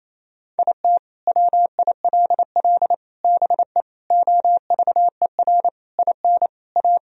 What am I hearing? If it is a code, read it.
Morse code, heard as a keyed tone, about 28 wpm, 700 Hz. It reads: IT WILL BE OVER IN A
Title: Sentences from Top 1000 Words - Review Part 2 - 28wpm